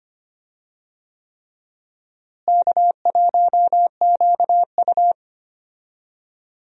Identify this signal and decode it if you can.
Morse code, heard as a keyed tone, about 25 wpm, 700 Hz. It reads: K1QU
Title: USA Call Signs - 4 Characters - Repeated 3x - 25wpm